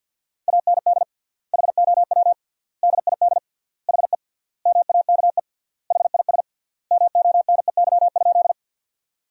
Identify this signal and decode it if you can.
Morse code, heard as a keyed tone, about 50 wpm, 700 Hz. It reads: AND HOW DID HE MAKE HIS MONEY?
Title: 3x Word-Spacing - Sentences from Top 400 Words - 50wpm